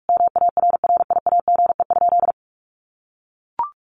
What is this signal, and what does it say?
Morse code, heard as a keyed tone, about 44 wpm, 700 Hz. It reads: MARRIAGE?
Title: Sentences from Top 1400 Words 45wpm